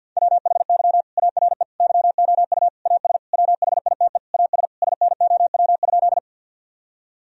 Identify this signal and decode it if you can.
Morse code, heard as a keyed tone, about 50 wpm, 700 Hz. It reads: WHY ARE YOU AS WHITE AS SNOW?
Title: Sentences from Top 1000 Words - Review Part 3 - 50wpm